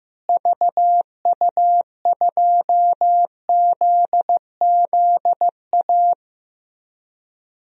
Morse code, heard as a keyed tone, about 15 wpm, 700 Hz. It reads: VU2ZZA
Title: DXCC-100 Call Signs - 6 Characters - Repeated 3x - 15wpm